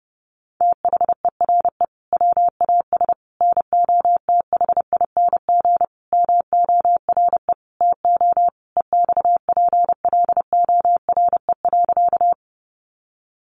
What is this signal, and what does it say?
Morse code, heard as a keyed tone, about 28 wpm, 700 Hz. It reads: THERE WAS NOTHING MORE TO EXPLORE.
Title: Sentences from Top 1100 Words 30wpm